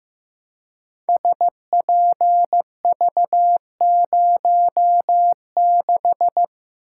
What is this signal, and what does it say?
Morse code, heard as a keyed tone, about 15 wpm, 700 Hz. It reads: SPV06
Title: Code Groups - 5 Letter:Numbers - 15wpm